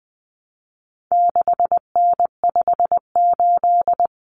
Morse code, heard as a keyed tone, about 20 wpm, 700 Hz. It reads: 6N58
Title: Common Cut Numbers - 4-Digit 20wpm